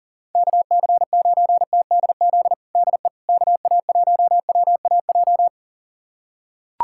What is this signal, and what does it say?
Morse code, heard as a keyed tone, about 40 wpm, 700 Hz. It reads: KC9TDZ DE KA1WAJ
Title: USA Callsign DE Callsign - 6 Characters - 40wpm